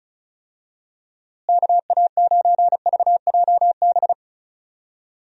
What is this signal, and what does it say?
Morse code, heard as a keyed tone, about 34 wpm, 700 Hz. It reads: KA9VJB
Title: USA Call Signs - 6 Characters - Repeated 2x - 35wpm